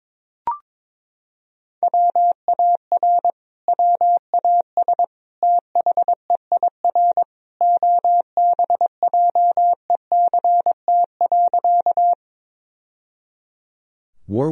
Morse code, heard as a keyed tone, about 22 wpm, 700 Hz. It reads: WAR WAS THEIR OBJECT.
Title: Sentences from Top 1000 Words - Review Part 3 - 22wpm